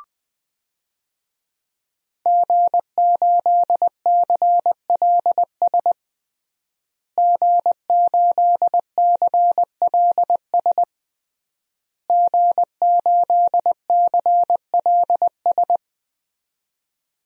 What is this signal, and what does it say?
Morse code, heard as a keyed tone, about 20 wpm, 700 Hz. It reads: G8CLS G8CLS G8CLS
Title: DXCC-100 Call Signs - 5 Characters - Repeated 3x - 20wpm